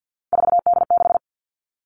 Morse code, heard as a keyed tone, about 50 wpm, 700 Hz. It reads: 4D6
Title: All Cut Numbers - 3-Digit 50wpm